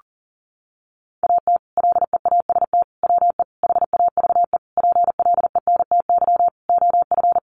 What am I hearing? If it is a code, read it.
Morse code, heard as a keyed tone, about 40 wpm, 700 Hz. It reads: AT LEAST WE HAVE PLENTY OF
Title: Sentences from Top 1300 Words 40wpm